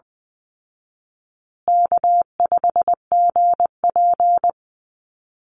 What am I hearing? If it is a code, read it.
Morse code, heard as a keyed tone, about 20 wpm, 700 Hz. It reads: K5GP
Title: Call Signs Repeated 3x - 20wpm